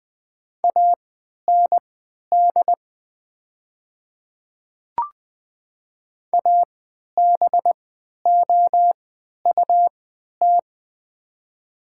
Morse code, heard as a keyed tone, about 20 wpm, spaced as for 11 wpm, 700 Hz. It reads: AND ABOUT
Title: 3x Character Spacing - Top 500 Words - 20wpm